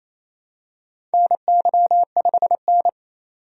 Morse code, heard as a keyed tone, about 27 wpm, 700 Hz. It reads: NY5N
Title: USA Call Signs - 4 Characters - 28wpm